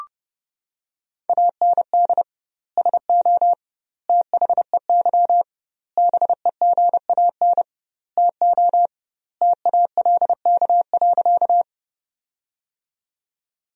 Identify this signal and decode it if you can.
Morse code, heard as a keyed tone, about 30 wpm, 700 Hz. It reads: AND SO THEY BEGAN TO TALK.
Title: 2x Word-Spacing - Sentences from Top 400 Words - 30wpm